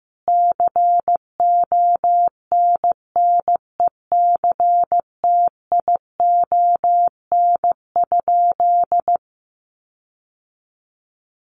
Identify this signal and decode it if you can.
Morse code, heard as a keyed tone, about 14 wpm, 700 Hz. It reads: CONNECTION?
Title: Sentences from Top 1300 Words 15wpm